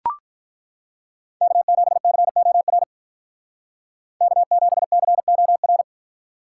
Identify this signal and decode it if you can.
Morse code, heard as a keyed tone, about 42 wpm, 700 Hz. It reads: K7COR K7COR
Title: Speed Racing - USA Call Signs 35wpm